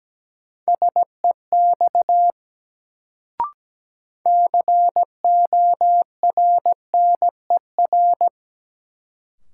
Morse code, heard as a keyed tone, about 17 wpm, 700 Hz. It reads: SEX CORNER
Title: Sentences from Top 1600 Words - New Word Drill Alt - 17wpm